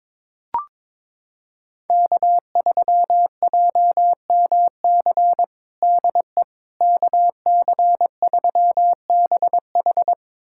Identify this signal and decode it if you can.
Morse code, heard as a keyed tone, about 22 wpm, 700 Hz. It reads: K3JMC DE KC3BH
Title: USA Callsign DE Callsign - 5 Characters - 22wpm